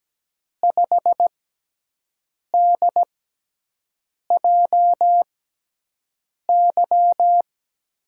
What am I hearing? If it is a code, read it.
Morse code, heard as a keyed tone, about 17 wpm, spaced as for 6 wpm, 700 Hz. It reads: 5DJY
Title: USA Call Signs - 6x Character Spacing - 17wpm